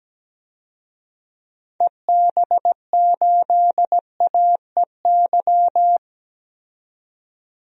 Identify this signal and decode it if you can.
Morse code, heard as a keyed tone, about 17 wpm, 700 Hz. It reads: EB8AEY
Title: DXCC-100 Call Signs - 6 Characters - 17wpm